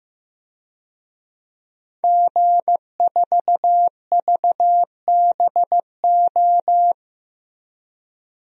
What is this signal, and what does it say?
Morse code, heard as a keyed tone, about 15 wpm, 700 Hz. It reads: G4VBO
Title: DXCC-100 Call Signs - 5 Characters - Repeated 3x - 15wpm